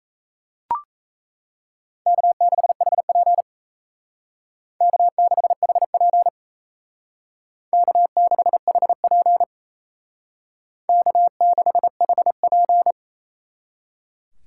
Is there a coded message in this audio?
Yes: Morse code from 2.1 to 12.9 s, heard as a keyed tone, about 35 wpm, 700 Hz. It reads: K6HP K6HP K6HP K6HP